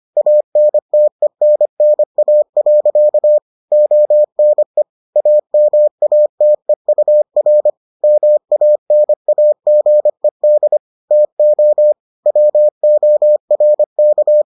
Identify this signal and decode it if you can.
Morse code, heard as a keyed tone, about 25 wpm, 600 Hz. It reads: ANTENNA. ONE AMATEUR MANAGED TO WORK